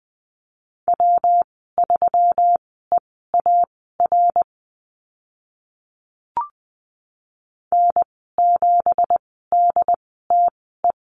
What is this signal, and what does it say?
Morse code, heard as a keyed tone, about 20 wpm, spaced as for 15 wpm, 700 Hz. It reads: W3EAR N7DTE
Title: 173 Lesson 28g - Number 9 - Call Signs 20wpm